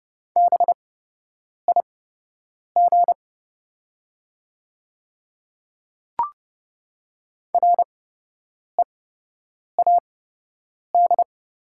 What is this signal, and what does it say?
Morse code, heard as a keyed tone, about 30 wpm, spaced as for 8 wpm, 700 Hz. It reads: BIG READ
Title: 8x Character Spacing - Top 300 Words - 30wpm